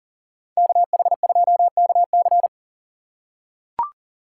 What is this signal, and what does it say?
Morse code, heard as a keyed tone, about 40 wpm, 700 Hz. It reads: KH2KC
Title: USA Call Signs - 5 Characters - 6x Character Spacing - 40wpm